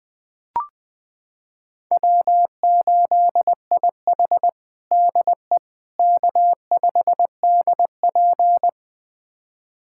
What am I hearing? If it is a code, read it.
Morse code, heard as a keyed tone, about 20 wpm, 700 Hz. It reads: W8IH DE K5DP
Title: USA Callsign DE Callsign - 4 Characters - 20wpm